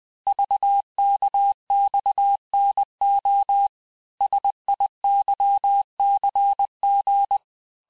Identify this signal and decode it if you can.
Morse code, heard as a keyed tone, about 20 wpm, 800 Hz. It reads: VKXNO SIYCG
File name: cours-du-lundi files / cours-1584-teste-chiffre-vitesse-19mpm